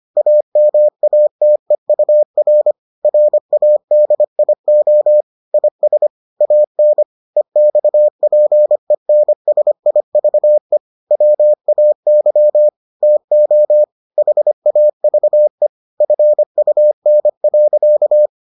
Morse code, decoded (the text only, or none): AMATEUR RADIO IS AN EXPENSIVE WAY TO HAVE FUN.